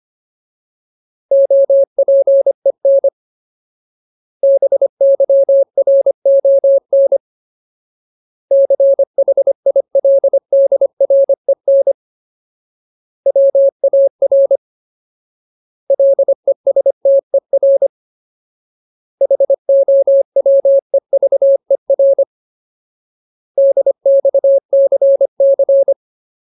Morse code, heard as a keyed tone, about 25 wpm, 550 Hz. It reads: OPEN BYRON CHILDREN WAR LESTER HOWEVER DXCC
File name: Ultimate Word List / 25WPM Ultimate